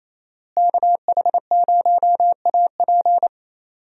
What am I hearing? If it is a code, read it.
Morse code, heard as a keyed tone, about 27 wpm, 700 Hz. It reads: KH0AP
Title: USA Call Signs - 5 Characters - Repeated 3x - 28wpm